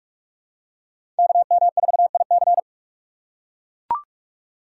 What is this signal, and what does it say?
Morse code, heard as a keyed tone, about 40 wpm, 700 Hz. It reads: KM4IC